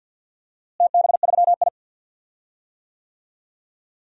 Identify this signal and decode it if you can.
Morse code, heard as a keyed tone, about 50 wpm, 700 Hz. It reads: TB3I
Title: DXCC-100 Call Signs - 4 Characters - Repeated 3x - 50wpm